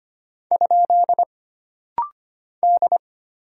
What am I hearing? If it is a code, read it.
Morse code, heard as a keyed tone, about 25 wpm, 700 Hz. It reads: ? D
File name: Single Character (Letter:Number:Punctuation:Prosign) - Rapid-Fire - 25wpm